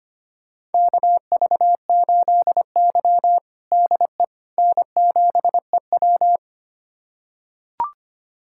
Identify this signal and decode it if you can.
Morse code, heard as a keyed tone, about 25 wpm, 700 Hz. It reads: KV8Y DE N7EW
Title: USA Callsign DE Callsign - 4 Characters - 25wpm